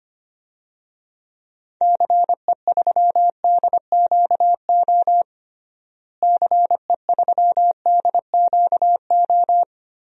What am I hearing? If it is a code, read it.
Morse code, heard as a keyed tone, about 25 wpm, 700 Hz. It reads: CE3DQO CE3DQO
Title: DXCC-100 Call Signs - 6 Characters - Repeated 2x - 25wpm